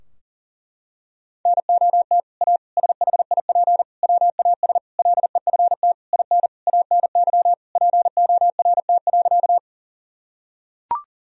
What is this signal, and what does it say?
Morse code, heard as a keyed tone, about 40 wpm, 700 Hz. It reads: NOT A SHIP WAS LEFT IN ANY PORT.
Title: Sentences from Top 1000 Words - Review Part 2 - 40wpm